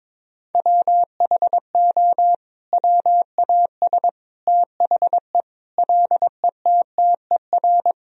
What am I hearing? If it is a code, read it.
Morse code, heard as a keyed tone, about 22 wpm, 700 Hz. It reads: WHO WAS THE LETTER
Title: Sentences from Top 300 Words and Abbreviations 22wpm